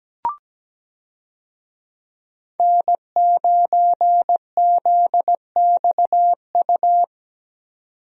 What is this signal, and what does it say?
Morse code, heard as a keyed tone, about 17 wpm, 700 Hz. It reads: N9ZXU
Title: USA Call Signs - 5 Characters - 17wpm